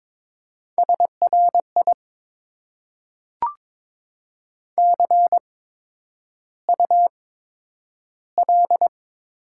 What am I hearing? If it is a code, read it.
Morse code, heard as a keyed tone, about 22 wpm, spaced as for 9 wpm, 690 Hz. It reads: SRI CUL
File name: Top 90 QSO Elements - 8x Character Spacing - 22wpm